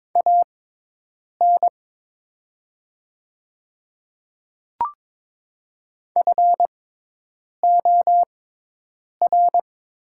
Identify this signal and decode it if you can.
Morse code, heard as a keyed tone, about 22 wpm, spaced as for 8 wpm, 700 Hz. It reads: AN FOR
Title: 6x Character Spacing - Top 100 Words - 22wpm